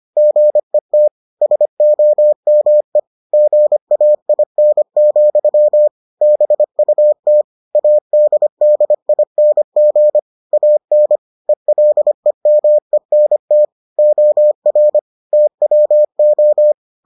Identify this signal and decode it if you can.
Morse code, heard as a keyed tone, about 25 wpm, 600 Hz. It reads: GET SOME GAIN, BUT ADDING AN ELEMENT OR TWO